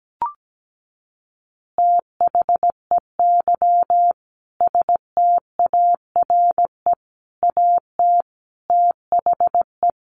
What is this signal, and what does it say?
Morse code, heard as a keyed tone, about 17 wpm, 700 Hz. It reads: THEY STARE AT THE
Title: Easy Phrases 17wpm